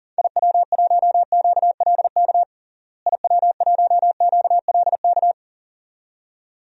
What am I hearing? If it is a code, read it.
Morse code, heard as a keyed tone, about 40 wpm, 700 Hz. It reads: IW1QLK IW1QLK
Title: DXCC-100 Call Signs - 6 Characters - Repeated 2x - 40wpm